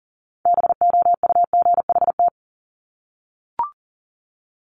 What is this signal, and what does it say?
Morse code, heard as a keyed tone, about 38 wpm, 700 Hz. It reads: BOUGHT
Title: Multi-letter Phonemes Repeated 3x 40wpm